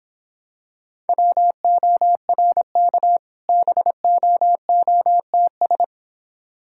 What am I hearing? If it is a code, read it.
Morse code, heard as a keyed tone, about 25 wpm, 700 Hz. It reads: WORK BOOTS